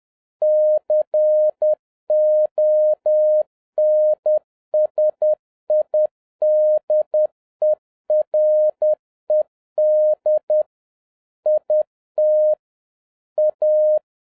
Morse code, heard as a keyed tone, about 10 wpm, 610 Hz. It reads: CONSIDERED IT A
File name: 2011-07-mp3 / n31 THE 3.11 2 10wpm